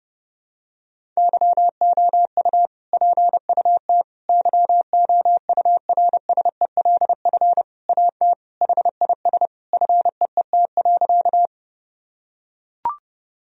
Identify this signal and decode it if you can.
Morse code, heard as a keyed tone, about 32 wpm, 700 Hz. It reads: YOU PUT YOURSELF AT HIS FEET.
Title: Sentences from Top 1000 Words - Review Part 1 - 30wpm